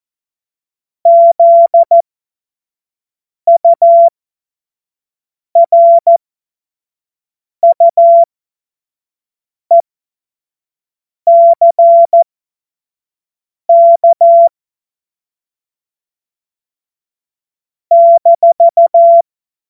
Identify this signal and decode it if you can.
Morse code, heard as a keyed tone, about 14 wpm, spaced as for 5 wpm, 680 Hz. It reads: ZURUECK -
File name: Morsekurs / Witze / Witz 21. Juni 2022 25 F70